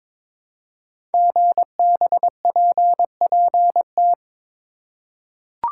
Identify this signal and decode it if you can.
Morse code, heard as a keyed tone, about 22 wpm, 700 Hz. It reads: GBPPT
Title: Code Groups - 5 Letters - 22wpm